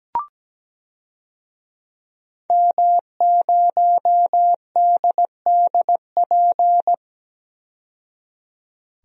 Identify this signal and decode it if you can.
Morse code, heard as a keyed tone, about 17 wpm, 700 Hz. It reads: M0DDP
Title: DXCC-100 Call Signs - 5 Characters - 17wpm